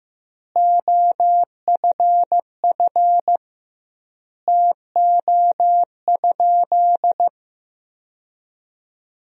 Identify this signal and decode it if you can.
Morse code, heard as a keyed tone, about 15 wpm, 700 Hz. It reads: OFF TO?
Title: 2x Word-Spacing - Sentences from Top 200 Words - 15wpm